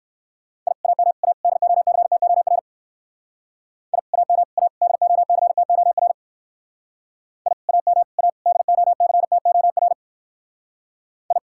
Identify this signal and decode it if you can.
Morse code, heard as a keyed tone, about 58 wpm, 700 Hz. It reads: I AM A DOCTOR I AM A DOCTOR I AM A DOCTOR I